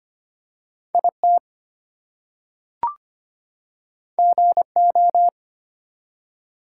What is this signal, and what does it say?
Morse code, heard as a keyed tone, about 25 wpm, 700 Hz. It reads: IT GO
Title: Two Letter Words - 25wpm